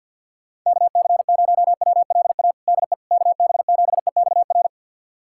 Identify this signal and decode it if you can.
Morse code, heard as a keyed tone, about 53 wpm, 700 Hz. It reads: KC0WLA DE KB7EXR
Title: USA Callsign DE Callsign 50wpm